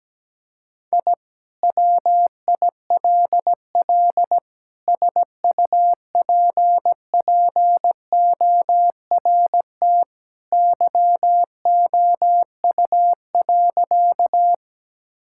Morse code, heard as a keyed tone, about 17 wpm, 700 Hz. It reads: I WILL SUPPORT YOU.
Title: Sentences from Top 1000 Words - Review Part 5 - 17wpm